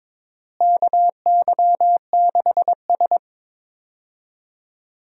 Morse code, heard as a keyed tone, about 22 wpm, 700 Hz. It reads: KY6S
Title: USA Call Signs - 4 Characters - Repeated 2x - 22wpm